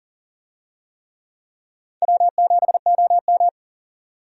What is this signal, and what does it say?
Morse code, heard as a keyed tone, about 37 wpm, 700 Hz. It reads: W7OM
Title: USA Call Signs - 4 Characters - 40wpm